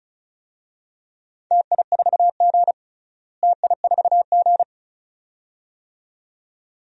Morse code, heard as a keyed tone, about 34 wpm, 700 Hz. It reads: TI4G TI4G
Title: DXCC-100 Call Signs - 4 Characters - Repeated 2x - 35wpm